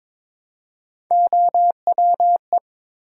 Morse code, heard as a keyed tone, about 22 wpm, 700 Hz. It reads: OWE
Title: Three Letter Words - 22wpm